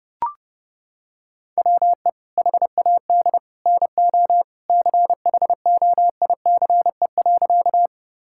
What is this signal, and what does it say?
Morse code, heard as a keyed tone, about 30 wpm, 700 Hz. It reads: WE HAD NO CHOICE.
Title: Sentences from Top 1000 Words - Review Part 6 - 30wpm